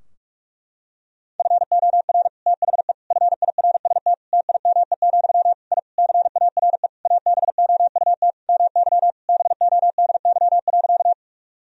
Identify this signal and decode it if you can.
Morse code, heard as a keyed tone, about 46 wpm, 700 Hz. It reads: FOR THE FIRST TIME, I CARE ABOUT MY BODY.
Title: Sentences from Top 1000 Words - Review Part 2 - 45wpm